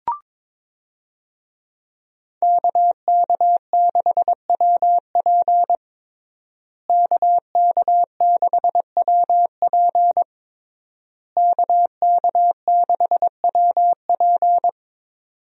Morse code, heard as a keyed tone, about 22 wpm, 700 Hz. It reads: KK6WP KK6WP KK6WP
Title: USA Call Signs - 5 Characters - Repeated 3x - 22wpm